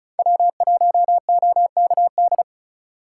Morse code, heard as a keyed tone, about 36 wpm, 700 Hz. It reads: W1OKD
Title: USA Call Signs - 5 Characters - 3x Character Spacing - 35wpm